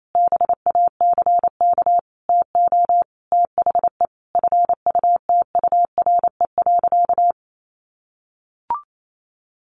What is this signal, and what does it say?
Morse code, heard as a keyed tone, about 28 wpm, 700 Hz. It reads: BACK TO THE FUTURE.
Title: A Sound Of Thunder 28wpm